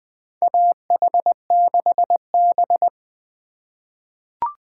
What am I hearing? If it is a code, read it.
Morse code, heard as a keyed tone, about 21 wpm, 700 Hz. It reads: AH6B
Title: USA Call Signs - 4 Characters - 2x Character Spacing - 20wpm